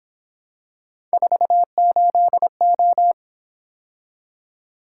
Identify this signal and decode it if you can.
Morse code, heard as a keyed tone, about 26 wpm, 700 Hz. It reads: 48O
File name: Speed-Racing - Three Letter-Number Combinations - 22wpm